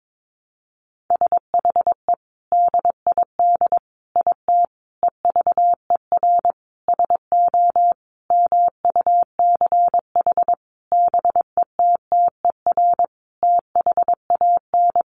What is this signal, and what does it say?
Morse code, heard as a keyed tone, about 22 wpm, 700 Hz. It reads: SHE DID IT EVER SO MUCH BETTER THAN
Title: Sentences from Top 400 Words 22wpm